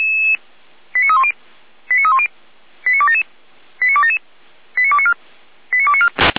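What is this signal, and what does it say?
Five-tone selective call (ZVEI), heard as quick runs of five tones, tones of about 70 ms. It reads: 89310 89310 89277 89277 89294 89294
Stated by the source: ZVEI five-tone selective call 89265 received by SDR.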